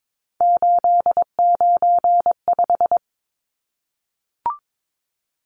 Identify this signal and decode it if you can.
Morse code, heard as a keyed tone, about 22 wpm, 700 Hz. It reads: 895
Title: Numbers - 3-Digit 22wpm